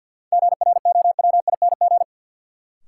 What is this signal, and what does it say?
Morse code, heard as a keyed tone, about 53 wpm, 700 Hz. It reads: GROWING
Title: Sets of 5 Words Encore - New Word Drill - 50wpm